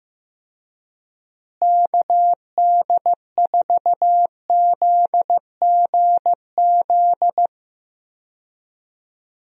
Morse code, heard as a keyed tone, about 15 wpm, 700 Hz. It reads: KD4ZGZ
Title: USA Call Signs - 6 Characters - Repeated 3x - 15wpm